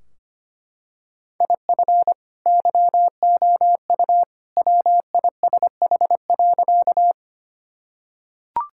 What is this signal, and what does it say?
Morse code, heard as a keyed tone, about 25 wpm, 700 Hz. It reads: IF YOU WISH.
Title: Sentences from Top 1600 Words 25wpm